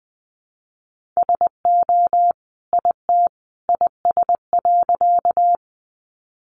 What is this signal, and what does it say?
Morse code, heard as a keyed tone, about 20 wpm, 700 Hz. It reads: SO IT IS.
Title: Sentences from Top 100 Words 20wpm